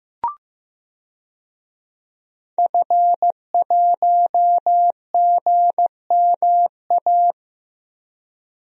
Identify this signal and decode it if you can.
Morse code, heard as a keyed tone, about 16 wpm, 700 Hz. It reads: F1GMA
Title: DXCC-100 Call Signs - 5 Characters - Repeated 3x - 15wpm